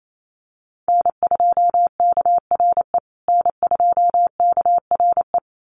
Morse code, heard as a keyed tone, about 28 wpm, 700 Hz. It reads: N2KRE N2KRE
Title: USA Call Signs Repeated 2x 28wpm